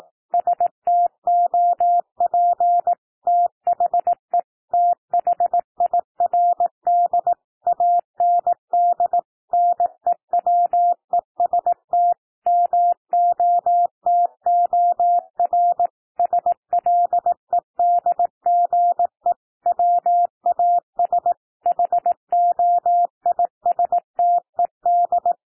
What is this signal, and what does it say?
Morse code, heard as a keyed tone, about 18 wpm, 700 Hz. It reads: STOP THE THIRD AND NEWEST MOTOR SLEDGE WAS HOISTED
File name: TEXT PASSAGE 18 wpm